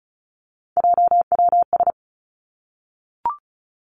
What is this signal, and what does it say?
Morse code, heard as a keyed tone, about 36 wpm, 700 Hz. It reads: JWS
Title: Three Challenging Letter-Number Combinations 35wpm